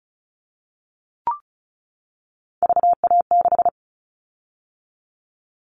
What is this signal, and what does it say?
Morse code, heard as a keyed tone, about 36 wpm, 690 Hz. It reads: VA6
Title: Callsign Prefixes for Top 25 DXCC - 35wpm